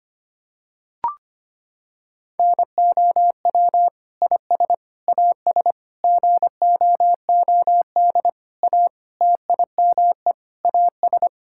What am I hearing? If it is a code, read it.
Morse code, heard as a keyed tone, about 25 wpm, 700 Hz. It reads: NOW IS AS GOOD A TIME AS
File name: practice / 25wpm / Sentences with 9-Word Phrases - 25wpm